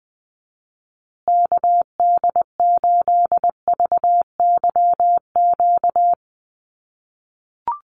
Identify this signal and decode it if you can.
Morse code, heard as a keyed tone, about 20 wpm, 700 Hz. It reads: KD8VYQ
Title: USA Call Signs Repeated 2x 20wpm